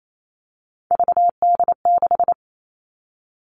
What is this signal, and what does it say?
Morse code, heard as a keyed tone, about 28 wpm, 700 Hz. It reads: VD6
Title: All Cut Numbers - 3-Digit 28wpm